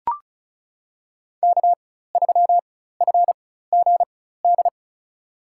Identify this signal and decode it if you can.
Morse code, heard as a keyed tone, about 34 wpm, spaced as for 16 wpm, 700 Hz. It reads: K3FGD
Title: USA Call Signs - 5 Characters - 4x Character Spacing - 35wpm